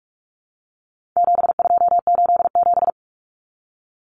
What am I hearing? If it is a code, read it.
Morse code, heard as a keyed tone, about 44 wpm, 700 Hz. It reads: 7287
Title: Common Cut Numbers - 4-Digit 45wpm